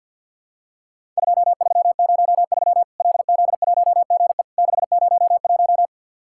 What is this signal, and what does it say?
Morse code, heard as a keyed tone, about 50 wpm, 700 Hz. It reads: 2303 LZ1GE 601
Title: Worked All Europe DX Contest - QTC Exchange - 50wpm